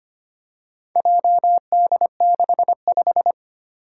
Morse code, heard as a keyed tone, about 25 wpm, 700 Hz. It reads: JD65
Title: Four Challenging Letter-Number Combinations 25wpm